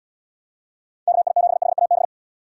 Morse code, heard as a keyed tone, about 78 wpm, 700 Hz. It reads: KI7HAB